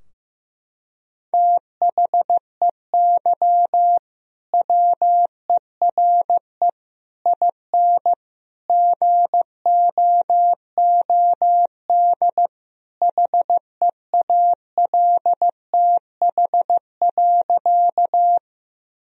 Morse code, heard as a keyed tone, about 15 wpm, 700 Hz. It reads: THEY WERE IN GOOD HEALTH.
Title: Sentences from Top 1000 Words - Review Part 3 - 15wpm